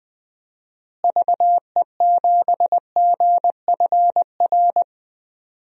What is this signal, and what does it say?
Morse code, heard as a keyed tone, about 20 wpm, 700 Hz. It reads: VE7GFR
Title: DXCC-100 Call Signs - 6 Characters - Repeated 3x - 20wpm